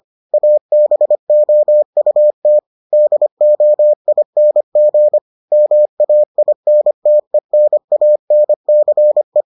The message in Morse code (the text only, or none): ABOUT DOING MAINTENANCE